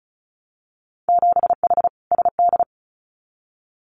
Morse code, heard as a keyed tone, about 36 wpm, 700 Hz. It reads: 7H SD
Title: Winter Field Day Exchanges Encore 35wpm